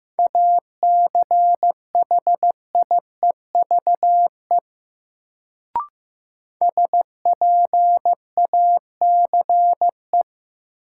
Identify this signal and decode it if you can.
Morse code, heard as a keyed tone, about 15 wpm, 700 Hz. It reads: ACHIEVE SPACE
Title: Sentences from Top 1000 Words - Review Part 5 - New Word Drill - 15wpm